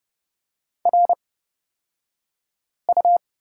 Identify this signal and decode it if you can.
Morse code, heard as a keyed tone, about 29 wpm, spaced as for 5 wpm, 700 Hz. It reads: RU
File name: Lesson 36d - BT - Two Characters 30wpm